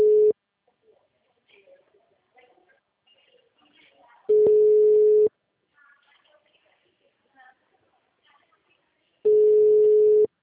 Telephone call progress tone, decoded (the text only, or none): ringback tone